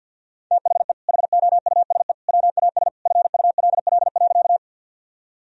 Morse code, heard as a keyed tone, about 52 wpm, 700 Hz. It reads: THE HOUSE WAS FULL.
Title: Sentences from Top 1000 Words - Review Part 3 - 50wpm